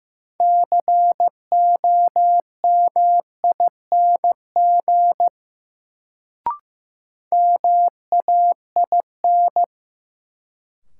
Morse code, heard as a keyed tone, about 15 wpm, 700 Hz. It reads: COMING MAIN
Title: Sentences from Top 1300 Words - New Word Drill Alt - 15wpm